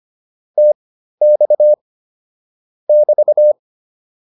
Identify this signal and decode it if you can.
Morse code, heard as a keyed tone, about 25 wpm, spaced as for 13 wpm, 600 Hz. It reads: TX <BT>